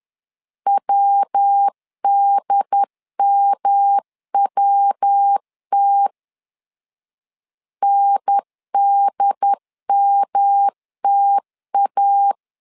Morse code, heard as a keyed tone, about 10 wpm, 800 Hz. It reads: WDMWT NDMTA
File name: LEZIONI / 3 MWRO / 3 MW 10wpm 1959